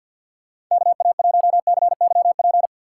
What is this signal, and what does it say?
Morse code, heard as a keyed tone, about 53 wpm, 700 Hz. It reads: KA1CYP
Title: USA Call Signs - 6 Characters - 3x Character Spacing - 50wpm